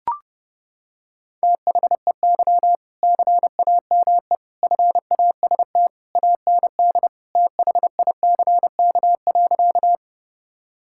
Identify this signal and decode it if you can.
Morse code, heard as a keyed tone, about 30 wpm, 700 Hz. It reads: THEY CAME FAST AND THICK.
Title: Sentences from Top 1600 Words 30wpm